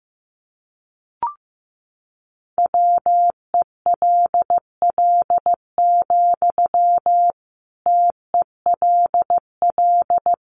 Morse code, heard as a keyed tone, about 15 wpm, 700 Hz. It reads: WELL, TELL
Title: Sentences from Top 200 Words 15wpm